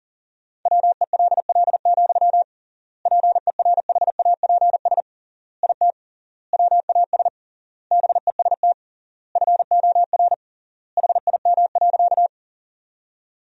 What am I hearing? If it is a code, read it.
Morse code, heard as a keyed tone, about 40 wpm, 700 Hz. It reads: WELL, PERHAPS IT WAS BEST FOR HIM.
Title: 3x Word-Spacing - Sentences from Top 500 Words - 40wpm